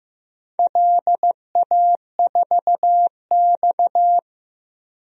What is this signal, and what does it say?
Morse code, heard as a keyed tone, about 15 wpm, 700 Hz. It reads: LA4X
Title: DXCC-100 Call Signs - 4 Characters - 15wpm